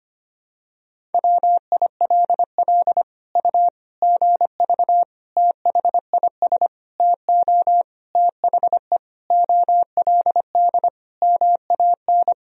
Morse code, heard as a keyed tone, about 25 wpm, 700 Hz. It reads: WILL U GV THIS TO THE OLD MAN
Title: Sentences from Top 200 Words and Abbreviations 25wpm